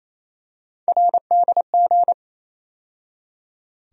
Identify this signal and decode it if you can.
Morse code, heard as a keyed tone, about 28 wpm, 700 Hz. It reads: RDG
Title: USA Call Sign Postfixes - 3 Letters 28wpm